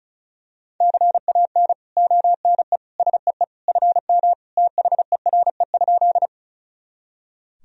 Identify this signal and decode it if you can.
Morse code, heard as a keyed tone, about 36 wpm, 700 Hz. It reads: CAN ONE SEE FM THERE?
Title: Sentences from Top 100 Words and Abbreviations 35wpm